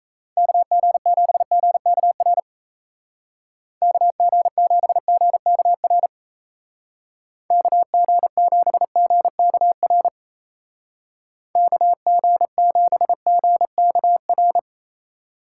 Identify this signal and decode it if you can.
Morse code, heard as a keyed tone, about 37 wpm, 700 Hz. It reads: KG7GKR KG7GKR KG7GKR KG7GKR